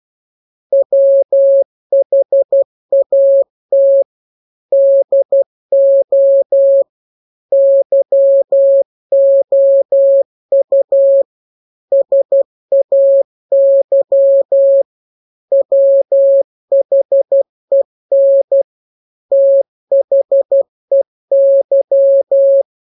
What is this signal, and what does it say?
Morse code, heard as a keyed tone, about 12 wpm, 550 Hz. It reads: WHAT DO YOU SAY WHEN THEY
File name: jokes / joke 21 12 0000